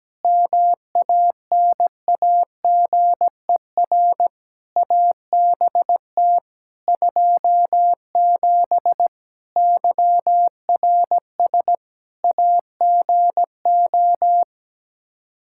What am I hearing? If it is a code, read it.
Morse code, heard as a keyed tone, about 17 wpm, 700 Hz. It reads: MANAGER ABT 27 YRS AGO
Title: Practice QSO - Exchange 3 - More Info - 17wpm